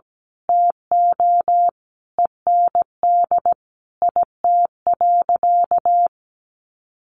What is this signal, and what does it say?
Morse code, heard as a keyed tone, about 17 wpm, 700 Hz. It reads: TO END IT.
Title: Sentences from Top 1200 Words 17wpm